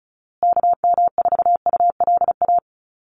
Morse code, heard as a keyed tone, about 34 wpm, 700 Hz. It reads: KM4ULA
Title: USA Call Signs Repeated 2x 35wpm